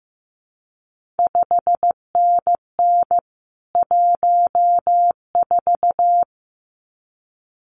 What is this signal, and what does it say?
Morse code, heard as a keyed tone, about 15 wpm, 700 Hz. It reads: 5NN 14
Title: CQ WW DX Contest Exchange - 15wpm